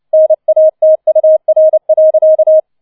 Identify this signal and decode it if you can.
Morse code, heard as a keyed tone, about 29 wpm, 630 Hz. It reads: NATUR.